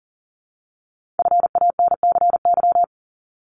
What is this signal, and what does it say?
Morse code, heard as a keyed tone, about 42 wpm, 700 Hz. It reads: FANCY
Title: Multi-letter Phonemes 40wpm